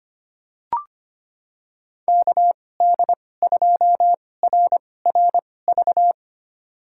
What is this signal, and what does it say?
Morse code, heard as a keyed tone, about 25 wpm, spaced as for 18 wpm, 700 Hz. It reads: KD2RRV
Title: USA Call Signs - 2x Character Spacing - 25wpm